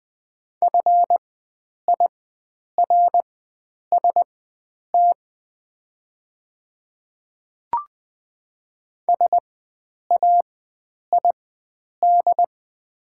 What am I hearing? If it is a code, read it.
Morse code, heard as a keyed tone, about 20 wpm, spaced as for 9 wpm, 690 Hz. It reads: FIRST SAID
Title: 4x Character Spacing - Top 100 Words - 20wpm